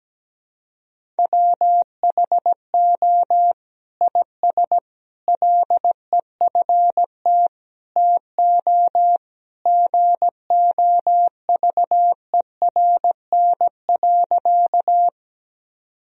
Morse code, heard as a keyed tone, about 17 wpm, 700 Hz. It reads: WHO IS LEFT TO GOVERN.
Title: Sentences from Top 1000 Words - Review Part 2 - 17wpm